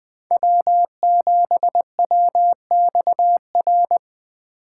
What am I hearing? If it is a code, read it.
Morse code, heard as a keyed tone, about 20 wpm, 700 Hz. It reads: W7WXR
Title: Speed Racing - USA 1x3 Call Signs 17wpm